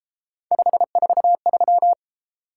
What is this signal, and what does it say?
Morse code, heard as a keyed tone, about 33 wpm, 700 Hz. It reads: 543